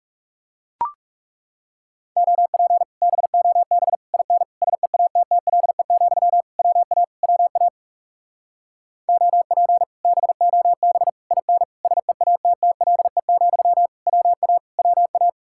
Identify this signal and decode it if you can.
Morse code, heard as a keyed tone, about 40 wpm, 700 Hz. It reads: OP BOB IN SEATTLE, WA WA OP BOB IN SEATTLE, WA WA